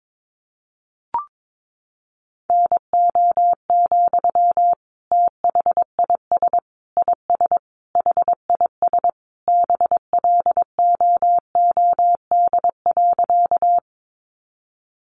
Morse code, heard as a keyed tone, about 22 wpm, 700 Hz. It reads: NO, THIS IS HIS BLOOD.
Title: Sentences from Top 1100 Words 22wpm